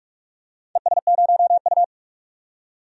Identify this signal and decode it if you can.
Morse code, heard as a keyed tone, about 47 wpm, 700 Hz. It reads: ES0U
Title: DXCC-100 Call Signs - 4 Characters - 45wpm